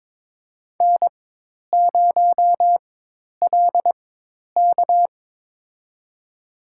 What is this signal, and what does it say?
Morse code, heard as a keyed tone, about 22 wpm, spaced as for 10 wpm, 700 Hz. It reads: N0LK
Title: USA Call Signs - 4 Characters - 4x Character Spacing - 22wpm